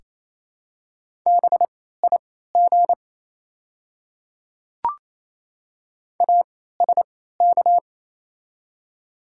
Morse code, heard as a keyed tone, about 28 wpm, spaced as for 16 wpm, 700 Hz. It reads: BIG ASK
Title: 3x Character Spacing - Top 200 Words - 28wpm